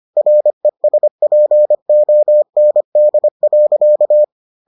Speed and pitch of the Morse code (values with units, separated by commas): 25 wpm, 600 Hz